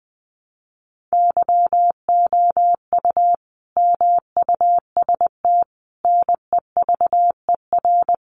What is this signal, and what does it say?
Morse code, heard as a keyed tone, about 20 wpm, 700 Hz. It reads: YOU MUST NEVER
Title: Sentences from Top 300 Words 20wpm